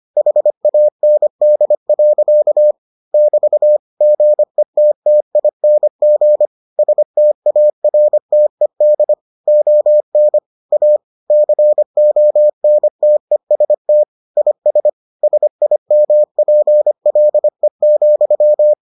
Morse code, heard as a keyed tone, about 25 wpm, 600 Hz. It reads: HAND. <BT> GETTING STARTED ON A CONTEST IS SIMPLE,